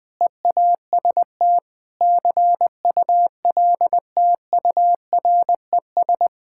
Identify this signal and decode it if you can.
Morse code, heard as a keyed tone, about 20 wpm, 700 Hz. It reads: EAST CULTURES
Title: Sentences with 4-Word Phrases - 20wpm